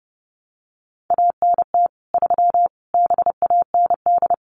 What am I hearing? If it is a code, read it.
Morse code, heard as a keyed tone, about 30 wpm, 700 Hz. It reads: ANT 3 BAND
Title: Common QSO Phrases 30wpm